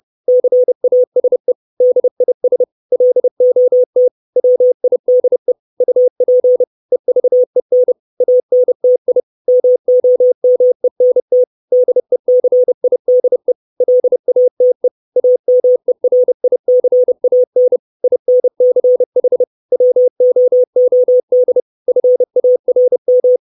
Morse code, heard as a keyed tone, about 30 wpm, 500 Hz. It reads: CASE DIS LOT WIDE UP EVEN ANTI MOMENT DECIDE LATE AMERICAN INCH WOOD FARM
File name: 632 Words 30WPM